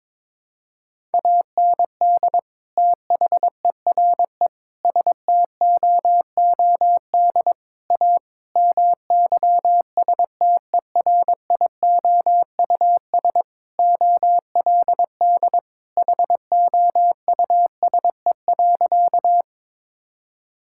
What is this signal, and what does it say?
Morse code, heard as a keyed tone, about 22 wpm, 700 Hz. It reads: AND THERE STOOD A MYSTERIOUS OLD HOUSE.
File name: practice / 22wpm / Sentences with 2-Word Phrases - 22wpm